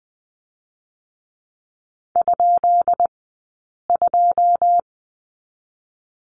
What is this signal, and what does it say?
Morse code, heard as a keyed tone, about 20 wpm, 700 Hz. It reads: ? 2
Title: 132 Lesson 22d - Question Mark - Two Characters 20wpm